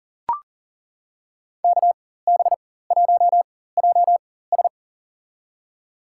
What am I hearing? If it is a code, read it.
Morse code, heard as a keyed tone, about 38 wpm, spaced as for 18 wpm, 700 Hz. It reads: KB1JS
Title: USA Call Signs - 5 Characters - 4x Character Spacing - 40wpm